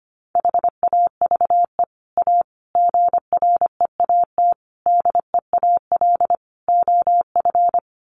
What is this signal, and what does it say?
Morse code, heard as a keyed tone, about 25 wpm, 700 Hz. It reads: HAVE A GREAT DEAL OF
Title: Sets of 5 Words Encore2 25wpm